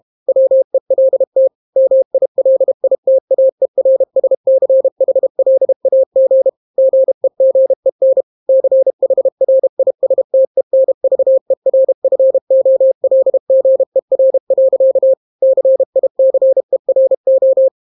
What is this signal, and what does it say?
Morse code, heard as a keyed tone, about 32 wpm, 530 Hz. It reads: WELT MILITAERSCHLAG GEGEN CHRISTENVERFOLGER. CICERO